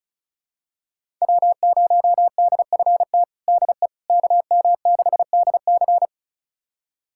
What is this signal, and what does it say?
Morse code, heard as a keyed tone, about 34 wpm, 700 Hz. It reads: W0DFT DE KM6DC
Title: USA Callsign DE Callsign - 5 Characters - 35wpm